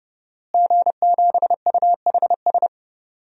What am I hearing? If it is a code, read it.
Morse code, heard as a keyed tone, about 30 wpm, 700 Hz. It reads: G7UHS
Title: DXCC-100 Call Signs - 5 Characters - Repeated 3x - 30wpm